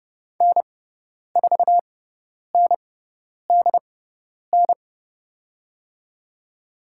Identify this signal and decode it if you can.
Morse code, heard as a keyed tone, about 30 wpm, spaced as for 10 wpm, 700 Hz. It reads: N4NDN
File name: Lesson 16g - Letter c - Call Signs 30wpm